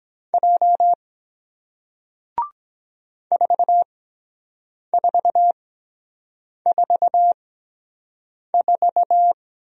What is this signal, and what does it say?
Morse code, heard as a keyed tone, about 23 wpm, spaced as for 7 wpm, 700 Hz. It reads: J 4444